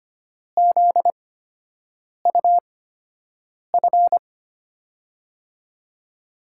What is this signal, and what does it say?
Morse code, heard as a keyed tone, about 25 wpm, spaced as for 7 wpm, 700 Hz. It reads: ZUF
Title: USA Call Signs - 6 Characters - 8x Character Spacing - 25wpm